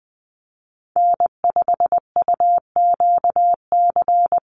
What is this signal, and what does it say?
Morse code, heard as a keyed tone, about 20 wpm, 700 Hz. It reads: N5UQC
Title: USA Call Signs 20wpm